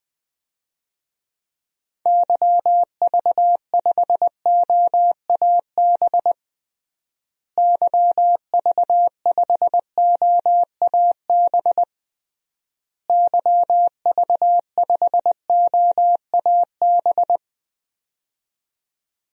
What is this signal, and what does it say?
Morse code, heard as a keyed tone, about 21 wpm, 700 Hz. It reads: YV5OAB YV5OAB YV5OAB
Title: DXCC-100 Call Signs - 6 Characters - Repeated 3x - 20wpm